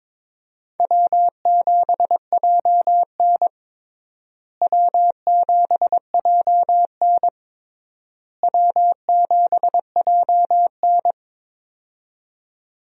Morse code, heard as a keyed tone, about 22 wpm, 700 Hz. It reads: W7JN W7JN W7JN
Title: USA Call Signs - 4 Characters - Repeated 3x - 22wpm